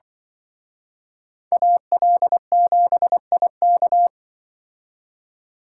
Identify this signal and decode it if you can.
Morse code, heard as a keyed tone, about 25 wpm, 700 Hz. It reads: AL7IK
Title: Speed Racing - USA 2x2 Call Signs 20wpm